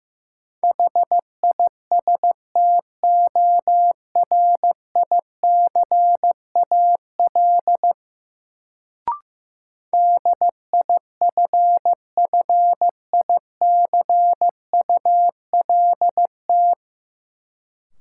Morse code, heard as a keyed tone, about 15 wpm, 700 Hz. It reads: HISTORICAL DIFFICULT
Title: Sentences from Top 600 Words - New Word Drill - 15wpm